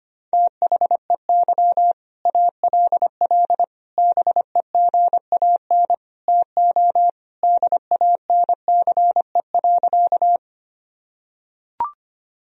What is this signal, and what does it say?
Morse code, heard as a keyed tone, about 25 wpm, 700 Hz. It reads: THEY ALL BEGAN TO DANCE.
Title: Sentences from Top 1000 Words - Review Part 4 - 25wpm